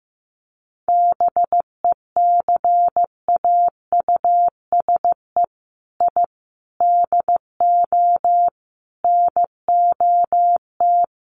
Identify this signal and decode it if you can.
Morse code, heard as a keyed tone, about 15 wpm, 700 Hz. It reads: BECAUSE I DO NOT
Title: Sets of 5 Words Encore 15wpm